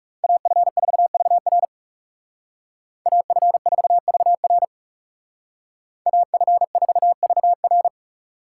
Morse code, heard as a keyed tone, about 39 wpm, 700 Hz. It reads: AF4VR AF4VR AF4VR